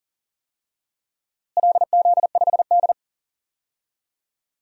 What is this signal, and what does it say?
Morse code, heard as a keyed tone, about 40 wpm, 700 Hz. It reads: LZ5D
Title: DXCC-100 Call Signs - 4 Characters - Repeated 2x - 40wpm